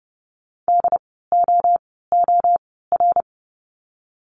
Morse code, heard as a keyed tone, about 30 wpm, 700 Hz. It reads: D O O R
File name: Lesson 16f - Letter c - Words 30wpm